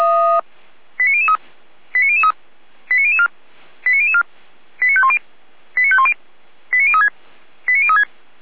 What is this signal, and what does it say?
Five-tone selective call (ZVEI), heard as quick runs of five tones, tones of about 70 ms. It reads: 89003 89003 89004 89004 89510 89510 89036 89036